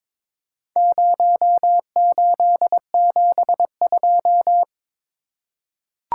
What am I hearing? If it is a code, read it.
Morse code, heard as a keyed tone, about 22 wpm, 700 Hz. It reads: 0872
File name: practice / 22wpm / Four Letter-Number Combinations - 22wpm